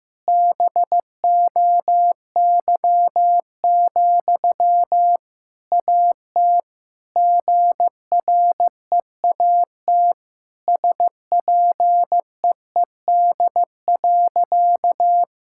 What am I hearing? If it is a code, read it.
Morse code, heard as a keyed tone, about 16 wpm, 700 Hz. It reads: BOY, AT GREAT SPEED.
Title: Sentences from Top 1000 Words - Review Part 5 - 15wpm